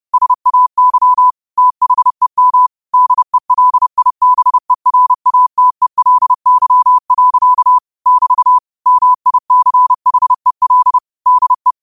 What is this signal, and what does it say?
Morse code, heard as a keyed tone, about 30 wpm, 1000 Hz. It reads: SAY THEM DELIBERATELY. <BT> MICHEL DE
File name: qotdfiles / 30wpm